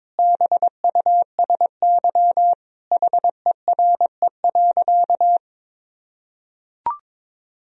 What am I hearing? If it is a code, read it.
Morse code, heard as a keyed tone, about 22 wpm, 700 Hz. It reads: BUSY HERE.
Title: Sentences from Top 1000 Words - Review Part 3 - 22wpm